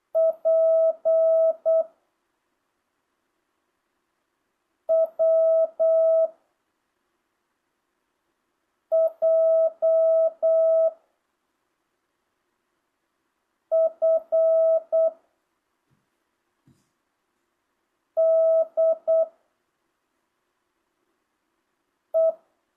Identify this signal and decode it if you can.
Morse code, heard as a keyed tone, about 8 wpm, 660 Hz. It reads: P W J F D E
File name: jh8cbh / 2025cw